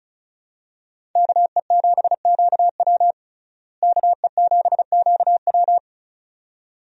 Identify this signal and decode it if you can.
Morse code, heard as a keyed tone, about 36 wpm, 700 Hz. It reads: KE7QW KE7QW
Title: USA Call Signs - 5 Characters - Repeated 2x - 35wpm